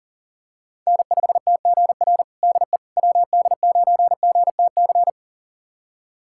Morse code, heard as a keyed tone, about 37 wpm, 700 Hz. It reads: N5TGR DE WD9GTC
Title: USA Callsign DE Callsign 40wpm